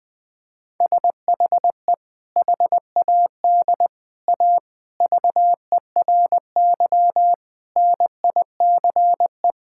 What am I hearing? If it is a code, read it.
Morse code, heard as a keyed tone, about 20 wpm, 700 Hz. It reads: SHE HAD A VERY NICE
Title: Sentences with 2-Word Phrases - 20wpm